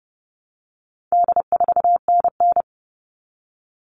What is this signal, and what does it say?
Morse code, heard as a keyed tone, about 30 wpm, 700 Hz. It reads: D4NN
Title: All Cut Numbers - 4-Digit 30wpm